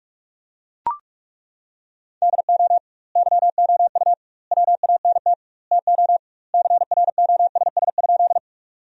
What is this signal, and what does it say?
Morse code, heard as a keyed tone, about 42 wpm, 700 Hz. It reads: DO YOU WANT TO CROSS?
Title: 2x Word-Spacing - Sentences from Top 300 Words - 45wpm